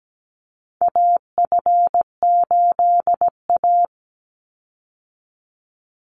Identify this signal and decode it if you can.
Morse code, heard as a keyed tone, about 17 wpm, 700 Hz. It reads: AF8A
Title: Call Signs - 17wpm